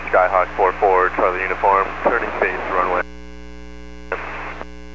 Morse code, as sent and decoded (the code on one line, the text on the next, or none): none